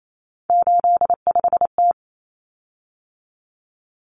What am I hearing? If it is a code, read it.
Morse code, heard as a keyed tone, about 28 wpm, 700 Hz. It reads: 85T
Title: Common Cut Numbers - 3-Digit 28wpm